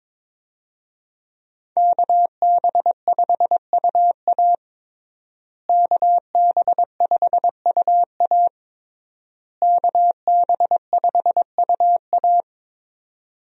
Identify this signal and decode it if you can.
Morse code, heard as a keyed tone, about 22 wpm, 700 Hz. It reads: KB5UA KB5UA KB5UA
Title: USA Call Signs - 5 Characters - Repeated 3x - 22wpm